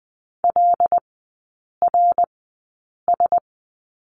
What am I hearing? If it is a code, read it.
Morse code, heard as a keyed tone, about 20 wpm, spaced as for 8 wpm, 700 Hz. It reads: LRS
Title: 087 Lesson 15e - Number 5 - Three Characters 20wpm